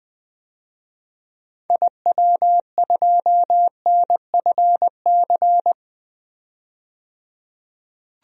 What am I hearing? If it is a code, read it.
Morse code, heard as a keyed tone, about 20 wpm, 700 Hz. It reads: IW2NFC
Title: DXCC-100 Call Signs - 6 Characters - 20wpm